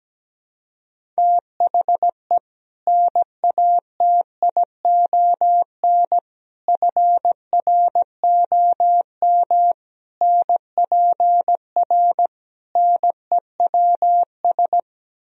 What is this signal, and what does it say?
Morse code, heard as a keyed tone, about 17 wpm, 700 Hz. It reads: THE NATION FROM NPR NEWS
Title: Speed-Racing - Sets of 5 Words 17wpm